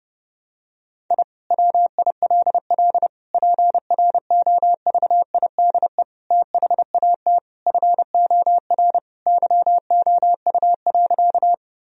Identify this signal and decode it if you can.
Morse code, heard as a keyed tone, about 30 wpm, 700 Hz. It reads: I WILL PROVIDE THAT FOR YOU.
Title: Sentences from Top 1000 Words - Review Part 3 - 30wpm